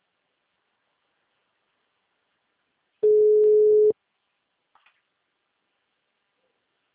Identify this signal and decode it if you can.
Ringback tone (Europe, 425 Hz).
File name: out- -2021- - - .13120